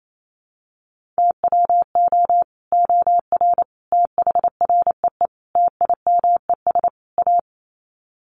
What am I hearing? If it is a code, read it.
Morse code, heard as a keyed tone, about 28 wpm, 700 Hz. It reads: TWO OR THREE TIMES A
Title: Sets of 5 Words Encore2 28wpm